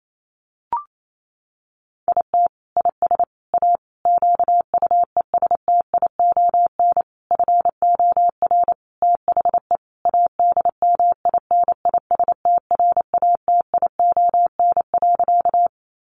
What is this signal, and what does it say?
Morse code, heard as a keyed tone, about 28 wpm, 700 Hz. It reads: IT IS A QUESTION FOR THE ADMINISTRATION.
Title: Sentences from Top 1300 Words 28wpm